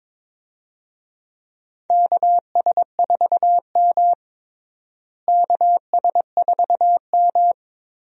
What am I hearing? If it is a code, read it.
Morse code, heard as a keyed tone, about 22 wpm, 700 Hz. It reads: KS4M KS4M
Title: USA Call Signs - 4 Characters - Repeated 3x - 22wpm